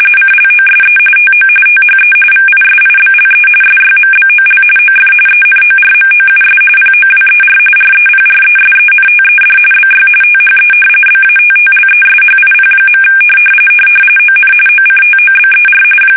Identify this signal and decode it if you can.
RTTY, about 45.45 baud. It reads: CLOSE TO THE VISIBLE SYMBOL OF ENORMOUS POWER REPRESENTED BY THE BATTLESHIP. ON OCTOBER
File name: fsk850 / battleshipnj